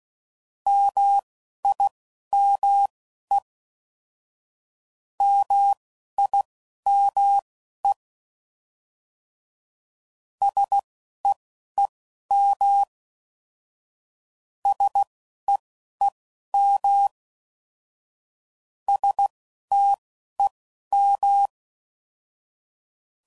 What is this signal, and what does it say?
Morse code, heard as a keyed tone, about 16 wpm, spaced as for 11 wpm, 800 Hz. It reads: MIME MIME SEEM SEEM STEM